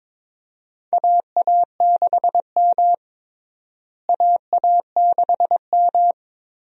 Morse code, heard as a keyed tone, about 22 wpm, 700 Hz. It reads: AA6M AA6M
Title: USA Call Signs - 4 Characters - Repeated 3x - 22wpm